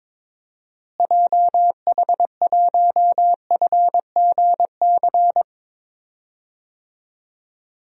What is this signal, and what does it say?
Morse code, heard as a keyed tone, about 22 wpm, 700 Hz. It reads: JH1FGC
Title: DXCC-100 Call Signs - 6 Characters - Repeated 3x - 22wpm